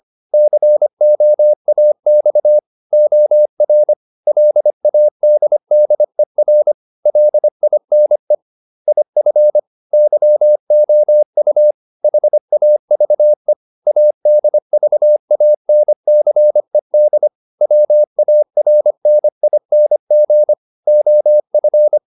Morse code, heard as a keyed tone, about 25 wpm, 600 Hz. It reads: COAX OR LADDER LINE IF YOU HAVE ADVANCED WARNING OF